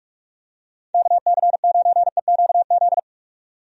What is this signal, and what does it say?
Morse code, heard as a keyed tone, about 41 wpm, 700 Hz. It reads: KC9EQZ
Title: USA Call Signs - 6 Characters - 45wpm